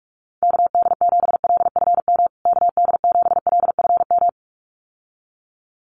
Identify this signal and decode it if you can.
Morse code, heard as a keyed tone, about 44 wpm, 700 Hz. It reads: KD7LFM KD7LFM
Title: USA Call Signs Repeated 2x 45wpm